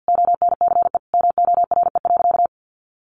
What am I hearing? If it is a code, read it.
Morse code, heard as a keyed tone, about 50 wpm, 700 Hz. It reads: ONCE MORE.
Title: Sentences from Top 1100 Words 50wpm